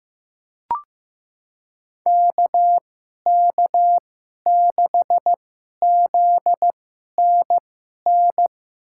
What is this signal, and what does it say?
Morse code, heard as a keyed tone, about 15 wpm, spaced as for 11 wpm, 700 Hz. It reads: KK6ZNN
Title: USA Call Signs - 6 Characters - 2x Character Spacing - 15wpm